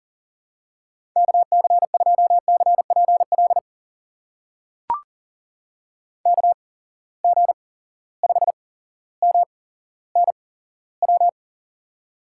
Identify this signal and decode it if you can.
Morse code, heard as a keyed tone, about 40 wpm, spaced as for 17 wpm, 700 Hz. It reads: KC2CPL KG5MNW
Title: USA Call Signs - 6 Characters - 8x Character Spacing - 40wpm